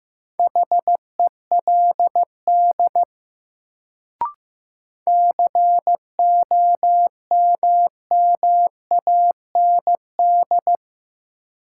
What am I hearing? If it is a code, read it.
Morse code, heard as a keyed tone, about 15 wpm, 700 Hz. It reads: HELD COMMAND
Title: Sentences from Top 1600 Words - New Word Drill Alt - 15wpm